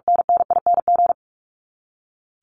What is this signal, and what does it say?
Morse code, heard as a keyed tone, about 44 wpm, 700 Hz. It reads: NNING
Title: ING Suffix - Top 100 Words with Base Words 45wpm